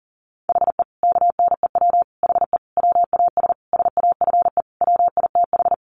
Morse code, heard as a keyed tone, about 40 wpm, 700 Hz. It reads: HE KNEW HE WAS SAFE WITH
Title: Sentences from Top 900 Words 40wpm